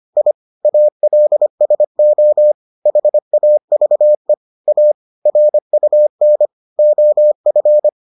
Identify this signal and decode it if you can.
Morse code, heard as a keyed tone, about 25 wpm, 600 Hz. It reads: I ALSO HAVE A RUN OF